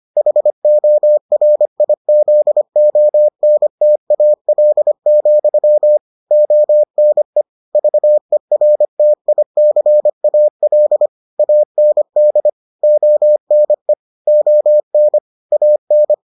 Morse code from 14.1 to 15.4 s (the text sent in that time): ON